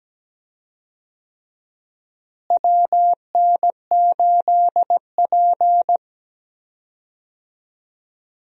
Morse code, heard as a keyed tone, about 17 wpm, 700 Hz. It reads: WN8P
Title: USA Call Signs - 4 Characters - Repeated 3x - 17wpm